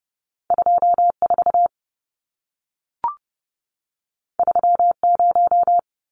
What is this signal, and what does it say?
Morse code, heard as a keyed tone, about 30 wpm, 700 Hz. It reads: 24 30
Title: Numbers - 2-Digit 30wpm